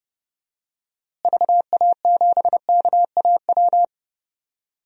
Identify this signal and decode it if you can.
Morse code, heard as a keyed tone, about 32 wpm, 700 Hz. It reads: VA7KAW
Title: DXCC-100 Call Signs - 6 Characters - Repeated 2x - 30wpm